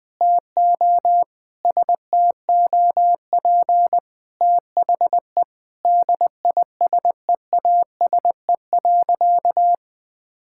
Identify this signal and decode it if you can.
Morse code, heard as a keyed tone, about 20 wpm, 700 Hz. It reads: TO STOP THE DISEASE.
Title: Sentences from Top 1000 Words - Review Part 6 - 20wpm